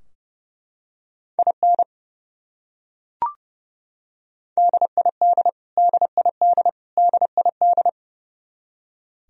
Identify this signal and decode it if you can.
Morse code, heard as a keyed tone, about 30 wpm, 700 Hz. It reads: IN DID DID DID
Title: Top 100 Words Repeated 30wpm